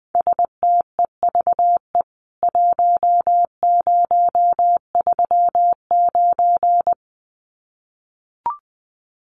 Morse code, heard as a keyed tone, about 20 wpm, 700 Hz. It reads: STEVE 1039
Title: CWT Exchange 20wpm